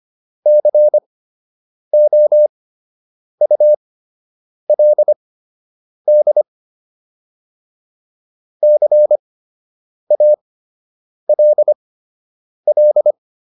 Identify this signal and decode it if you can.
Morse code, heard as a keyed tone, about 25 wpm, spaced as for 8 wpm, 600 Hz. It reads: COULD CALL